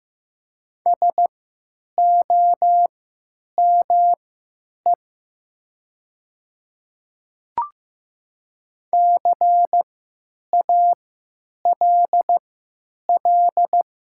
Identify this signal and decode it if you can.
Morse code, heard as a keyed tone, about 15 wpm, spaced as for 9 wpm, 700 Hz. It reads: SOME CALL
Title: 3x Character Spacing - Top 200 Words - 15wpm